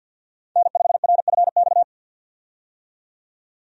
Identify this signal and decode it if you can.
Morse code, heard as a keyed tone, about 52 wpm, 700 Hz. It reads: N5RFX
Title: USA Call Signs - 5 Characters - 50wpm